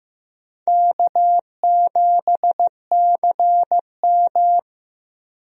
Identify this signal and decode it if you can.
Morse code, heard as a keyed tone, about 15 wpm, 700 Hz. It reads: K7CM
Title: USA Call Signs - 4 Characters - 15wpm